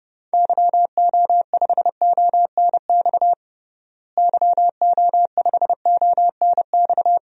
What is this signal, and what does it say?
Morse code, heard as a keyed tone, about 32 wpm, 700 Hz. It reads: YO5ONX YO5ONX
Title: DXCC-100 Call Signs - 6 Characters - Repeated 2x - 30wpm